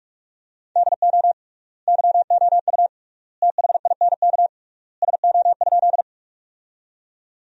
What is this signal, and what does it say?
Morse code, heard as a keyed tone, about 44 wpm, 700 Hz. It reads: DO YOU THINK SO?
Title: 3x Word-Spacing - Sentences from Top 200 Words - 45wpm